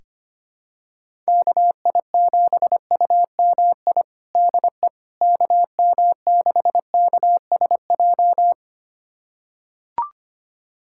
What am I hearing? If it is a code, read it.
Morse code, heard as a keyed tone, about 25 wpm, 700 Hz. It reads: KI7UMI DE KM6KSJ
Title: USA Callsign DE Callsign - 6 Characters - 25wpm